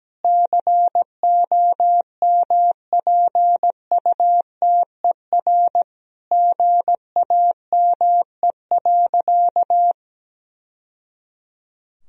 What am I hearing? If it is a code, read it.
Morse code, heard as a keyed tone, about 17 wpm, 700 Hz. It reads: COMPUTER GAME.
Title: Sentences from Top 1000 Words - Review Part 3 - 17wpm